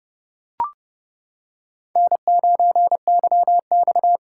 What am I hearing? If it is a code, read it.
Morse code, heard as a keyed tone, about 28 wpm, 700 Hz. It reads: N9YX